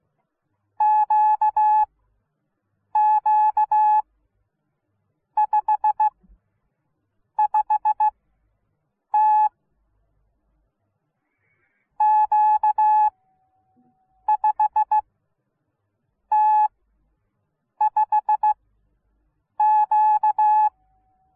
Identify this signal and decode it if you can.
Morse code, heard as a keyed tone, about 16 wpm, spaced as for 6 wpm, 850 Hz. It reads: QQ55T Q5T5Q